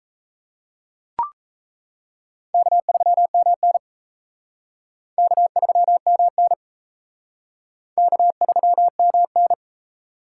Morse code, heard as a keyed tone, about 35 wpm, 700 Hz. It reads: K3MN K3MN K3MN
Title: Speed Racing - USA 1x2 Call Signs 28wpm